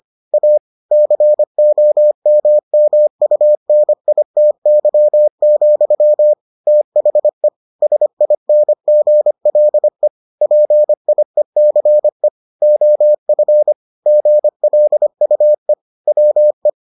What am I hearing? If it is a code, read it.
Morse code, heard as a keyed tone, about 25 wpm, 600 Hz. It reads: A COMMUNITY, THE SINGLE PIECE OF GLUE WE